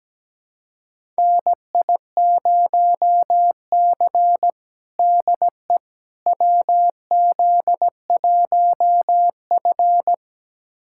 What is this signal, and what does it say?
Morse code, heard as a keyed tone, about 17 wpm, 700 Hz. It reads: NI0C DE WZ1F
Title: USA Callsign DE Callsign - 4 Characters - 17wpm